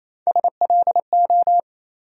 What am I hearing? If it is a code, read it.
Morse code, heard as a keyed tone, about 28 wpm, 700 Hz. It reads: SLO
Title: USA Call Sign Postfixes - 3 Letters 28wpm